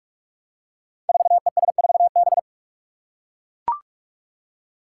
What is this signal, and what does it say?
Morse code, heard as a keyed tone, about 44 wpm, 700 Hz. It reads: 4ES4B
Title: Five Challenging Letter-Number Combinations 45wpm